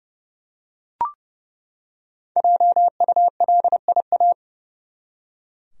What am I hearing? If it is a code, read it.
Morse code, heard as a keyed tone, about 30 wpm, 700 Hz. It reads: JULIA
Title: USA Common Names - Max 5 Letters - 30wpm